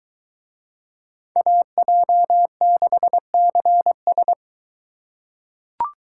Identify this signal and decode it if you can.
Morse code, heard as a keyed tone, about 23 wpm, 700 Hz. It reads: AJ6CS